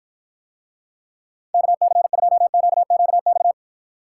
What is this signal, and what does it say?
Morse code, heard as a keyed tone, about 49 wpm, 700 Hz. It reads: KC2QQX